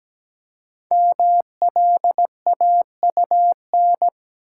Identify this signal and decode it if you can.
Morse code, heard as a keyed tone, about 17 wpm, 700 Hz. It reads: MLAUN
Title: Code Groups - 5 Letters - 17wpm